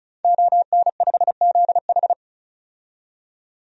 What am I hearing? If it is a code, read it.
Morse code, heard as a keyed tone, about 36 wpm, 700 Hz. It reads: ON5ZH
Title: DXCC-100 Call Signs - 5 Characters - 35wpm